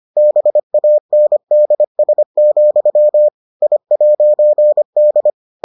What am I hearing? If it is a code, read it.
Morse code, heard as a keyed tone, about 25 wpm, 600 Hz. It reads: BANDS, I'D